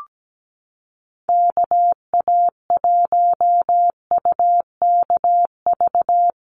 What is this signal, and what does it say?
Morse code, heard as a keyed tone, about 17 wpm, 700 Hz. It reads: KA1UKV
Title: USA Call Signs 17wpm